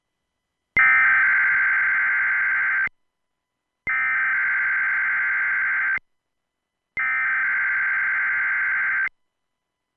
Emergency Alert System (SAME) tones (header bursts), heard as harsh buzzing bursts. It is heard from 0.8 to 9.1 s.